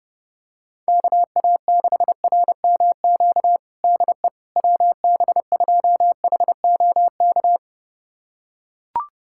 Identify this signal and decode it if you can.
Morse code, heard as a keyed tone, about 30 wpm, 700 Hz. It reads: KA6RMQ DE WB2HOK
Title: USA Callsign DE Callsign - 6 Characters - 30wpm